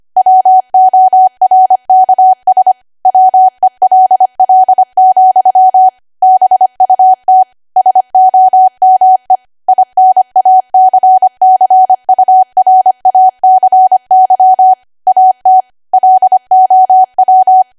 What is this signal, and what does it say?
Morse code, heard as a keyed tone, about 25 wpm, 750 Hz. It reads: WORKS WELL, BUT SOME INACCURACY AT LOW
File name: Archive / 25 WPM / 25WPM